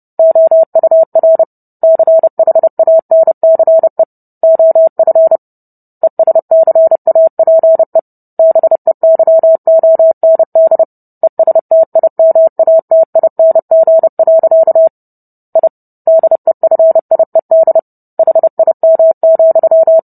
Morse code, heard as a keyed tone, about 30 wpm, 650 Hz. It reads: OUR CHANCE OF ESCAPE BEYOND ESTIMATING. I DEFIED HIM,